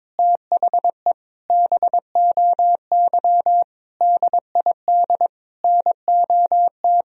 Morse code, heard as a keyed tone, about 22 wpm, 700 Hz. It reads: THE BOY DID NOT
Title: Sentences from Top 1000 Words - Review Part 3 - 22wpm